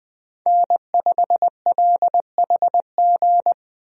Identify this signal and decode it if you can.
Morse code, heard as a keyed tone, about 20 wpm, 700 Hz. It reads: N5LHG
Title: USA Call Signs - 5 Characters - 20wpm